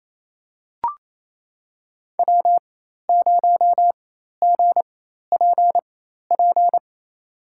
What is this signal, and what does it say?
Morse code, heard as a keyed tone, about 28 wpm, spaced as for 13 wpm, 700 Hz. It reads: W0GPP
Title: USA Call Signs - 5 Characters - 4x Character Spacing - 28wpm